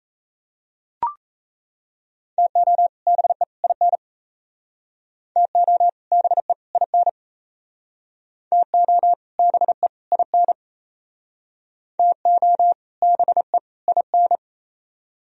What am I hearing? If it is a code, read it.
Morse code, heard as a keyed tone, about 34 wpm, 700 Hz. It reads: TO BE IN TO BE IN TO BE IN TO BE IN
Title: Speed-Racing - Sets of 3 Words 28wpm